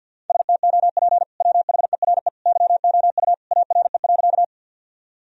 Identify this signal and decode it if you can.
Morse code, heard as a keyed tone, about 52 wpm, 700 Hz. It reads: STOP WHERE YOU ARE.
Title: Sentences from Top 1000 Words - Review Part 1 - 50wpm